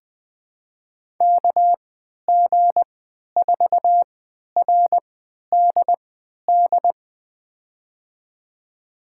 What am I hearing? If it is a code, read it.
Morse code, heard as a keyed tone, about 20 wpm, spaced as for 11 wpm, 700 Hz. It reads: KG4RDD
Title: USA Call Signs - 6 Characters - 3x Character Spacing - 20wpm